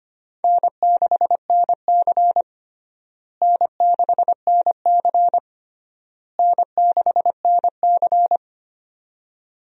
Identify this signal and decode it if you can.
Morse code, heard as a keyed tone, about 25 wpm, 700 Hz. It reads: N6NC N6NC N6NC
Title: USA Call Signs - 4 Characters - Repeated 3x - 25wpm